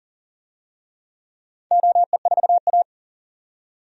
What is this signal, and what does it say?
Morse code, heard as a keyed tone, about 40 wpm, 700 Hz. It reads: OE4A
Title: DXCC-100 Call Signs - 4 Characters - 40wpm